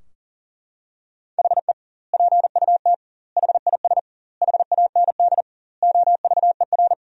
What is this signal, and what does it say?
Morse code, heard as a keyed tone, about 40 wpm, 700 Hz. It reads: HE PUT HIS HAND OVER
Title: 2x Word-Spacing - Sentences from Top 400 Words - 40wpm